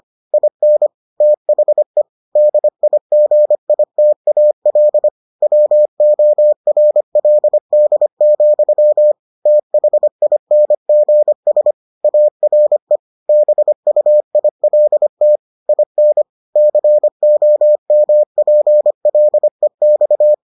Morse code, heard as a keyed tone, about 25 wpm, 600 Hz. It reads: IN THE DIGITAL WORLD, THINGS ARE BUILT IN COMPLEX